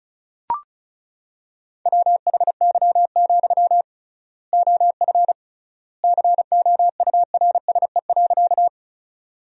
Morse code, heard as a keyed tone, about 36 wpm, 700 Hz. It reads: WHY, OF COURSE.
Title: 3x Word-Spacing - Sentences from Top 500 Words - 35wpm